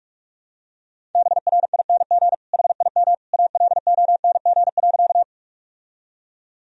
Morse code, heard as a keyed tone, about 43 wpm, 700 Hz. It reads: BRING HIM ALONG.
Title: Sentences from Top 1000 Words - Review Part 3 - 45wpm